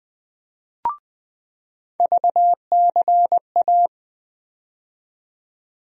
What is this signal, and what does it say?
Morse code, heard as a keyed tone, about 20 wpm, 700 Hz. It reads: VCA
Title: USA Call Sign Postfixes - 3 Letters 20wpm